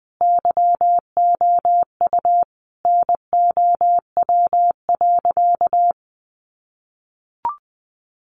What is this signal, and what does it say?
Morse code, heard as a keyed tone, about 20 wpm, 700 Hz. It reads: YOU NOW.
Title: Sentences from Top 1200 Words 20wpm